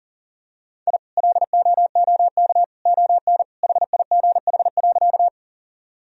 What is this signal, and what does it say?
Morse code, heard as a keyed tone, about 40 wpm, 700 Hz. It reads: I LOOK ON HIGH.
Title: Sentences from Top 1000 Words - Review Part 1 - 40wpm